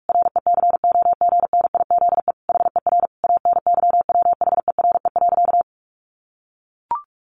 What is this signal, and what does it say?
Morse code, heard as a keyed tone, about 44 wpm, 700 Hz. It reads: RECOGNIZE HER ANYWHERE.
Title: Sentences from Top 1400 Words 45wpm